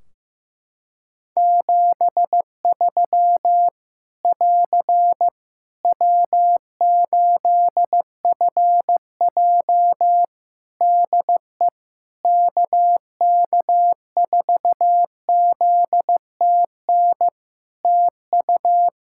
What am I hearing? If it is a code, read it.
Morse code, heard as a keyed tone, about 15 wpm, 700 Hz. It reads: 73 <AR> W8FJ DE KK4ZTN TU
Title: Practice QSO - Exchange 4 - Ending - 15wpm